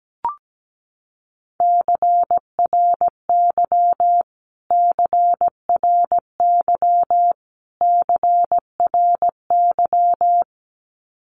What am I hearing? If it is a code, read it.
Morse code, heard as a keyed tone, about 17 wpm, 700 Hz. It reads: CRY CRY CRY
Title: Top 500 Words Repeated 17wpm